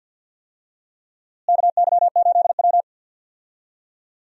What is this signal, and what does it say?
Morse code, heard as a keyed tone, about 51 wpm, 700 Hz. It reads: KY8W
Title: USA Call Signs - 4 Characters - Repeated 3x - 50wpm